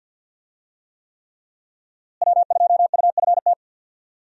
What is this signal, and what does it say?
Morse code, heard as a keyed tone, about 51 wpm, 700 Hz. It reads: W2UFT
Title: USA Call Signs - 5 Characters - Repeated 3x - 50wpm